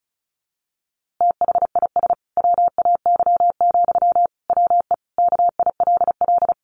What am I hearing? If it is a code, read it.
Morse code, heard as a keyed tone, about 34 wpm, 700 Hz. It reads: THIS WAY, WE KILL
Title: A Sound Of Thunder 35wpm